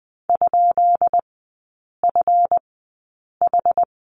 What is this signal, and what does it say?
Morse code, heard as a keyed tone, about 20 wpm, spaced as for 8 wpm, 700 Hz. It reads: ?FH
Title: 178 Lesson 29e - Slash - Three Characters 20wpm